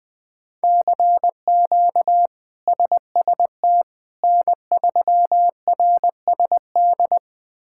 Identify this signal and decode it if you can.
Morse code, heard as a keyed tone, about 20 wpm, 700 Hz. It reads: CQ SST N3RSD
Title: K1USN SST QSO Practice - 20wpm